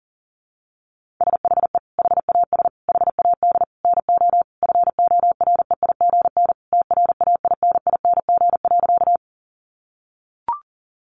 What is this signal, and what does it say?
Morse code, heard as a keyed tone, about 40 wpm, 700 Hz. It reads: SHE HAS HAD NO FOREIGN TRAINING.
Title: Sentences from Top 1400 Words 40wpm